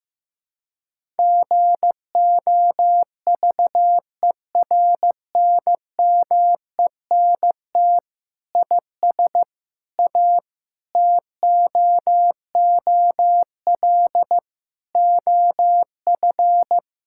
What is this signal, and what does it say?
Morse code, heard as a keyed tone, about 15 wpm, 700 Hz. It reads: GOVERNMENT IS A TOOL OF
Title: Sentences from Top 1000 Words - Review Part 6 - 15wpm